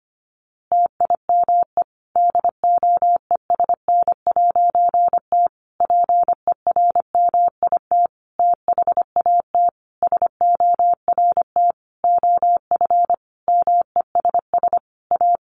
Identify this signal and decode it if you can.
Morse code, heard as a keyed tone, about 25 wpm, 700 Hz. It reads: TIME DOESN'T PERMIT THAT SORT OF MESS A
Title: A Sound Of Thunder 25wpm